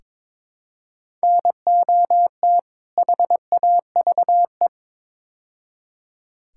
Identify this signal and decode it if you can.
Morse code, heard as a keyed tone, about 22 wpm, 700 Hz. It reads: NOT HAVE
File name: practice / 22wpm / Build A Phrase - 5-Word Phrases - 22wpm